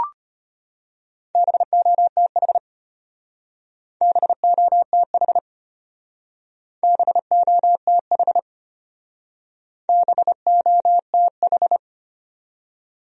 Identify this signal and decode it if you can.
Morse code, heard as a keyed tone, about 31 wpm, 700 Hz. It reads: BOTH BOTH BOTH BOTH
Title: Speed-Racing Top 500 Words 25wpm